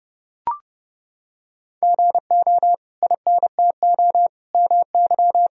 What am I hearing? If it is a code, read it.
Morse code, heard as a keyed tone, about 32 wpm, 700 Hz. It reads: GO INTO MY
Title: Sentences from Top 1000 Words - Review Part 1 - 30wpm